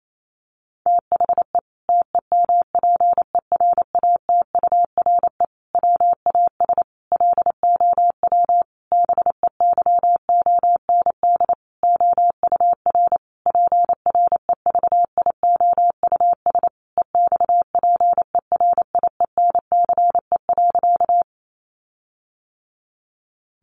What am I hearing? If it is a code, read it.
Morse code, heard as a keyed tone, about 28 wpm, 700 Hz. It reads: THE TEMPERATURE WAS LOW BEYOND OUR PREVIOUS EXPERIENCE.
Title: Sentences from Top 1500 Words 28wpm